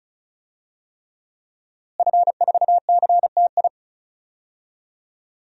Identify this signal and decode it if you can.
Morse code, heard as a keyed tone, about 36 wpm, 700 Hz. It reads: F4CTI
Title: DXCC-100 Call Signs - 5 Characters - Repeated 3x - 35wpm